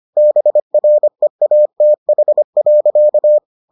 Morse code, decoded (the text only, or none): BREATH.